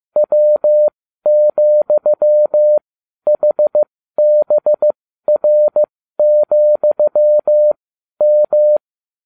Morse code, heard as a keyed tone, about 15 wpm, spaced as for 12 wpm, 600 Hz. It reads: W,HBR,M